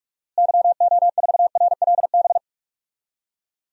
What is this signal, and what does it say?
Morse code, heard as a keyed tone, about 42 wpm, 700 Hz. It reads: YO4RLB
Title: DXCC-100 Call Signs - 6 Characters - Repeated 2x - 45wpm